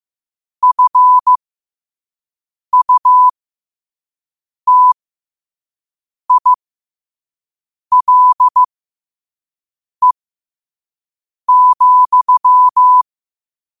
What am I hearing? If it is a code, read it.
Morse code, heard as a keyed tone, about 15 wpm, spaced as for 5 wpm, 1000 Hz. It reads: FUTILE,
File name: 5wpm